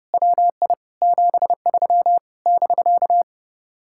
Morse code, heard as a keyed tone, about 30 wpm, 700 Hz. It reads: WI 73 <BK>
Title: Speed-Racing Common QSO Phrases Encore - 30wpm